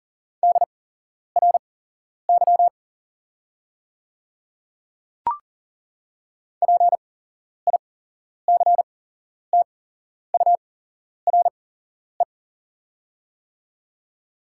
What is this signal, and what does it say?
Morse code, heard as a keyed tone, about 41 wpm, spaced as for 11 wpm, 700 Hz. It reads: DRY PICTURE
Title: 8x Character Spacing - Top 500 Words - 40wpm